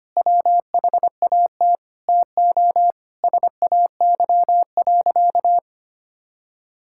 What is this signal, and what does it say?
Morse code, heard as a keyed tone, about 25 wpm, 700 Hz. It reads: WHAT TO SAY.
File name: practice / 25wpm / Sentences with 4-Word Phrases - 25wpm